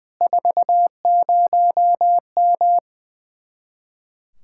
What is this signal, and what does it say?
Morse code, heard as a keyed tone, about 20 wpm, 700 Hz. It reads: 40M
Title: Speed-Racing Common QSO Phrases Encore - 20wpm